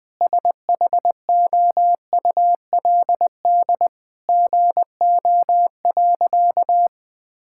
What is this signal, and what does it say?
Morse code, heard as a keyed tone, about 19 wpm, 700 Hz. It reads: SHOULD GO.
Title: Sentences with 4-Word Phrases - Spoken Beginning - 20wpm